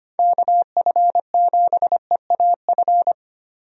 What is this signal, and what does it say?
Morse code, heard as a keyed tone, about 25 wpm, 700 Hz. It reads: KF7EAF
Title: USA Call Signs - 6 Characters - 25wpm